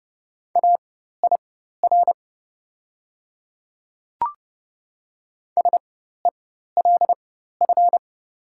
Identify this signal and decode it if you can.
Morse code, heard as a keyed tone, about 30 wpm, spaced as for 14 wpm, 700 Hz. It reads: AIR SELF
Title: 4x Character Spacing - Top 300 Words - 30wpm